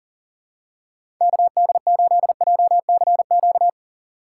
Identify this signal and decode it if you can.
Morse code, heard as a keyed tone, about 40 wpm, 700 Hz. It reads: KD8JCQ
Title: USA Call Signs - 4x Character Spacing - 40wpm